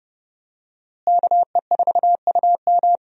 Morse code, heard as a keyed tone, about 31 wpm, 700 Hz. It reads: KE4UM
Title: USA Call Signs - 5 Characters - Repeated 2x - 30wpm